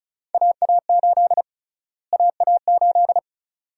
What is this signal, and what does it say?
Morse code, heard as a keyed tone, about 34 wpm, 700 Hz. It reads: AA8 AA8
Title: USA Call Sign Prefixes - Repeated 2x - 35wpm